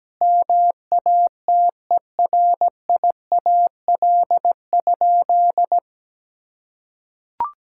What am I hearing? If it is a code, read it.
Morse code, heard as a keyed tone, about 17 wpm, 700 Hz. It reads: MATERIAL?
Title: Sentences from Top 1000 Words - Review Part 5 - 17wpm